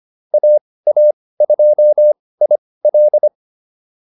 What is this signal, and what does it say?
Morse code, heard as a keyed tone, about 25 wpm, spaced as for 18 wpm, 600 Hz. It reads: AA2IL